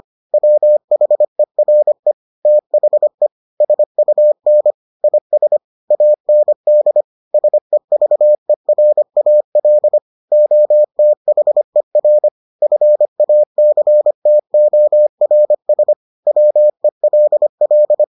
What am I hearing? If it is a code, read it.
Morse code, heard as a keyed tone, about 25 wpm, 600 Hz. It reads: WHERE THE SUN IS AND SEVERAL OTHER FACTORS WELL